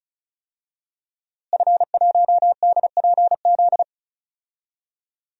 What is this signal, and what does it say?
Morse code, heard as a keyed tone, about 34 wpm, 700 Hz. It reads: F1DPZ
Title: DXCC-100 Call Signs - 5 Characters - Repeated 3x - 35wpm